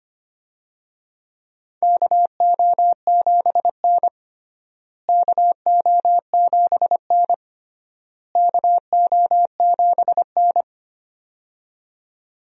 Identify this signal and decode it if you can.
Morse code, heard as a keyed tone, about 25 wpm, 700 Hz. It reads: KO7N KO7N KO7N
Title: USA Call Signs - 4 Characters - Repeated 3x - 25wpm